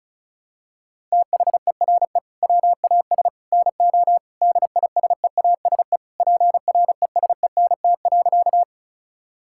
Morse code, heard as a keyed tone, about 36 wpm, 700 Hz. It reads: THERE WAS NO DISEASE PRESENT.
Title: Sentences from Top 1000 Words - Review Part 6 - 35wpm